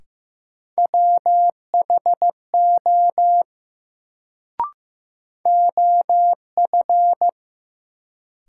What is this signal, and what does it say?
Morse code, heard as a keyed tone, about 15 wpm, 700 Hz. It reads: WHO OF
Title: Sentences from Top 100 Words - New Word Drill - 15wpm